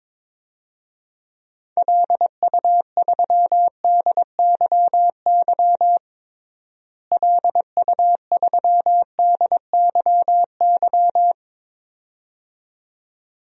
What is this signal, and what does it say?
Morse code, heard as a keyed tone, about 22 wpm, 700 Hz. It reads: LU3DYY LU3DYY
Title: DXCC-100 Call Signs - 6 Characters - Repeated 2x - 22wpm